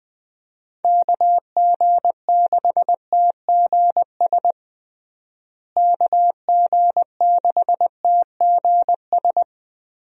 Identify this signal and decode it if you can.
Morse code, heard as a keyed tone, about 20 wpm, 700 Hz. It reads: KG6TGS KG6TGS
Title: USA Call Signs - 6 Characters - Repeated 2x - 20wpm